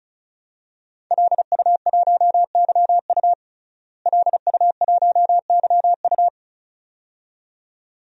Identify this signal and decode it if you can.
Morse code, heard as a keyed tone, about 36 wpm, 700 Hz. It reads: LU1YU LU1YU
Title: DXCC-100 Call Signs - 5 Characters - Repeated 2x - 35wpm